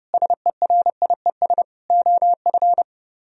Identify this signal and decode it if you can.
Morse code, heard as a keyed tone, about 29 wpm, 700 Hz. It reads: SERIES OF